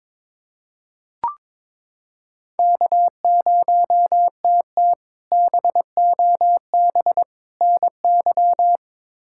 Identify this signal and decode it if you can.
Morse code, heard as a keyed tone, about 22 wpm, 700 Hz. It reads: K0TT BOB NY
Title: K1USN SST QSO Practice - 22wpm